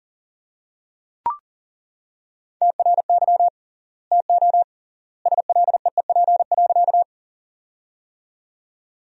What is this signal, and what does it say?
Morse code, heard as a keyed tone, about 40 wpm, 700 Hz. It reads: TRY TO SLEEP.
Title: 3x Word-Spacing - Sentences from Top 500 Words - 40wpm